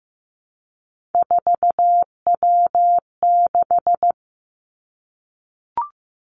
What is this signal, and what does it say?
Morse code, heard as a keyed tone, about 15 wpm, 700 Hz. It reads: 4W6
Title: Three Challenging Letter-Number Combinations 15wpm